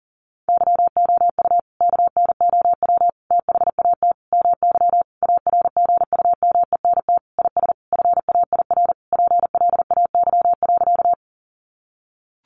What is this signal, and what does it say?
Morse code, heard as a keyed tone, about 37 wpm, 700 Hz. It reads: YOU KNOW THAT MY ARGUMENT IS FAIR PLAY.
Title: Sentences from Top 1400 Words 40wpm